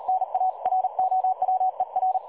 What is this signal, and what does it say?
Morse code, heard as a keyed tone, about 38 wpm, 750 Hz. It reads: NA ROWER
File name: dlazaawansowanych / nagraniazpasma / websdr recording 3565.0kHz